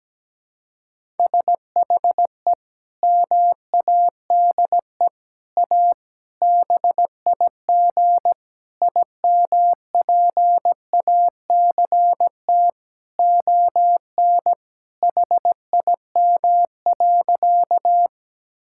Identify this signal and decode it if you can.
Morse code, heard as a keyed tone, about 17 wpm, 700 Hz. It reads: SHE MADE A BIG IMPACT ON HIM.
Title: Sentences from Top 1000 Words - Review Part 6 - 17wpm